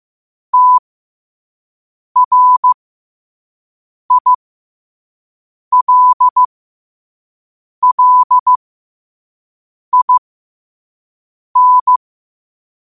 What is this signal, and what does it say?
Morse code, heard as a keyed tone, about 15 wpm, spaced as for 5 wpm, 1000 Hz. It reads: TRILLIN
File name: qotdfiles / 5wpm